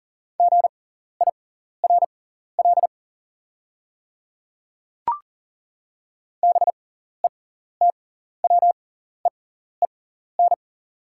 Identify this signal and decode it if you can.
Morse code, heard as a keyed tone, about 40 wpm, spaced as for 14 wpm, 690 Hz. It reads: GIRL BETWEEN
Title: 6x Character Spacing - Top 500 Words - 40wpm